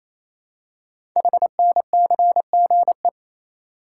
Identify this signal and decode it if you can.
Morse code, heard as a keyed tone, about 27 wpm, 700 Hz. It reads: HNCGE
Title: Code Groups - 5 Letters - 28wpm